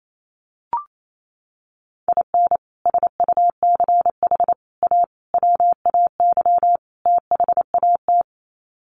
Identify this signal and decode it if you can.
Morse code, heard as a keyed tone, about 28 wpm, 700 Hz. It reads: IN SUCH A WAY THAT
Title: Sets of 5 Words 28wpm